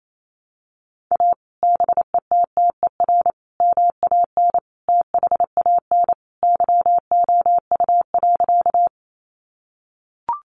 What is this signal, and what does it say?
Morse code, heard as a keyed tone, about 28 wpm, 700 Hz. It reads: A BETTER MAN THAN YOU.
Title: Sentences from Top 400 Words 28wpm